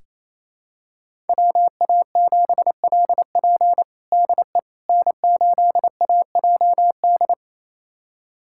Morse code, heard as a keyed tone, about 28 wpm, 700 Hz. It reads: WA7LP DE N8AJD
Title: USA Callsign DE Callsign - 5 Characters - 28wpm